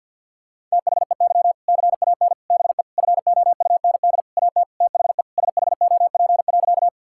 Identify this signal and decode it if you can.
Morse code, heard as a keyed tone, about 50 wpm, 700 Hz. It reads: THEY CAN BE FOUND AT THE SHOP.
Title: Sentences from Top 1600 Words 50wpm